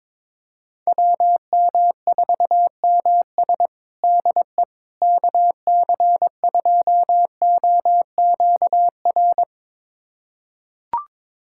Morse code, heard as a keyed tone, about 22 wpm, 700 Hz. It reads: WM4MS DE KC2OQR
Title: USA Callsign DE Callsign 22wpm